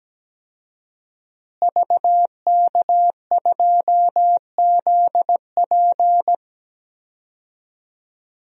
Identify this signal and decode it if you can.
Morse code, heard as a keyed tone, about 17 wpm, 700 Hz. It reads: VK2ZP
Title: DXCC-100 Call Signs - 5 Characters - Repeated 2x - 17wpm